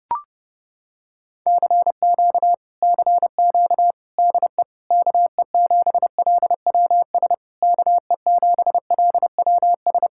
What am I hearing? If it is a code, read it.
Morse code, heard as a keyed tone, about 29 wpm, 700 Hz. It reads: CQ CQ DE KE7LWS KE7LWS